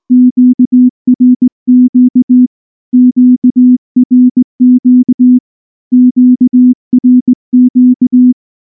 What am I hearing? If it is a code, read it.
Morse code, heard as a keyed tone, about 19 wpm, 250 Hz. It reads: QRQ QRQ QRQ
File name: morse / morse-training-20WPM-1